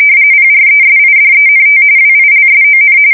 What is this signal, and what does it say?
RTTY, about 45.45 baud. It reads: PRESIDENT WILSON
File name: fsk170 / usnavy4